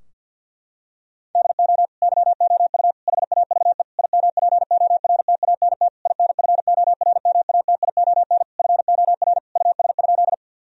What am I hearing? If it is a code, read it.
Morse code, heard as a keyed tone, about 50 wpm, 700 Hz. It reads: DO YOU HAVE IMPORTANT INFORMATION FOR US?
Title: Sentences from Top 1000 Words - Review Part 3 - 50wpm